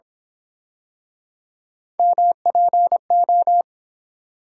Morse code, heard as a keyed tone, about 27 wpm, 700 Hz. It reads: MPO
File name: Speed-Racing - Three Letter-Number Combinations - 22wpm